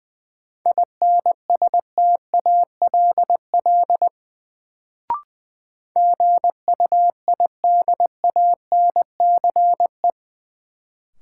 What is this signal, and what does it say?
Morse code, heard as a keyed tone, about 20 wpm, 700 Hz. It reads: INSTALL GUIDANCE
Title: Sentences from Top 1600 Words - New Word Drill Alt - 20wpm